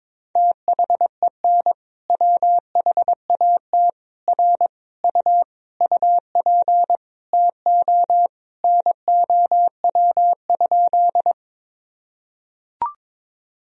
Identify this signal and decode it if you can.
Morse code, heard as a keyed tone, about 22 wpm, 700 Hz. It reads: THEN WHAT R U UP TO NOW?
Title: Sentences from Top 100 Words and Abbreviations 22wpm